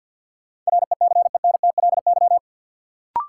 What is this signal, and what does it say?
Morse code, heard as a keyed tone, about 52 wpm, 700 Hz. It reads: RECENTLY
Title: Sentences from Top 1200 Words - New Word Drill - 50wpm